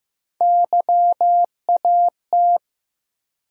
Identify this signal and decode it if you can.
Morse code, heard as a keyed tone, about 15 wpm, 700 Hz. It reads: YAT
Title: USA Call Sign Postfixes - 3 Letters 15wpm